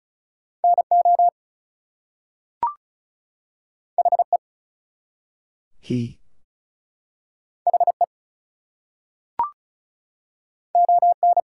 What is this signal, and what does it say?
Morse code, heard as a keyed tone, about 36 wpm, 700 Hz. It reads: NO HE HE ON
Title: Two Letter Words - 35wpm